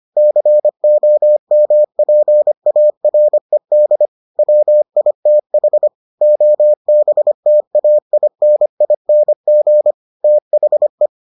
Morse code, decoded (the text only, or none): COMPARED WITH OBTAINING THE